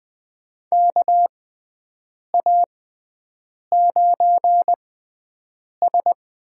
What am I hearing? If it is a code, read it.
Morse code, heard as a keyed tone, about 20 wpm, spaced as for 7 wpm, 700 Hz. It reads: KA9S
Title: USA Call Signs - 6 Characters - 6x Character Spacing - 20wpm